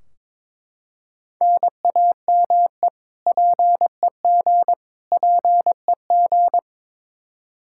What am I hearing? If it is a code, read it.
Morse code, heard as a keyed tone, about 22 wpm, 700 Hz. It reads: NAME PEG PEG
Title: Practice QSO - Exchange 1 - Beginning - 22wpm